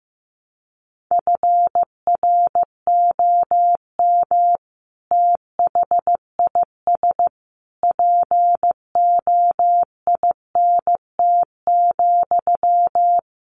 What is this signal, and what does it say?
Morse code, heard as a keyed tone, about 14 wpm, 700 Hz. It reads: FROM THIS POINT,
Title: Sentences from Top 1200 Words 15wpm